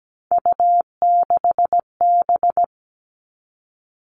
Morse code, heard as a keyed tone, about 17 wpm, 700 Hz. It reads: U6B
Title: All Cut Numbers - 3-Digit 17wpm